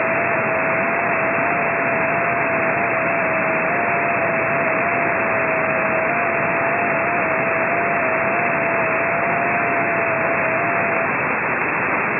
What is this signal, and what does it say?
FT8.